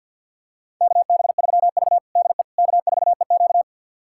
Morse code, heard as a keyed tone, about 51 wpm, 700 Hz. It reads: KB3V DE K4EQ
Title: USA Callsign DE Callsign - 4 Characters - 50wpm